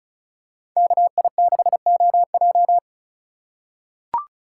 Morse code, heard as a keyed tone, about 36 wpm, 700 Hz. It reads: KI6OJ
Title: USA Call Signs - 5 Characters - 4x Character Spacing - 35wpm